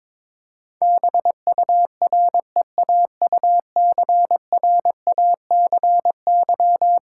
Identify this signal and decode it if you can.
Morse code, heard as a keyed tone, about 22 wpm, 700 Hz. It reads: BUREAUCRACY
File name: practice / 22wpm / Eleven Letter Words - 22wpm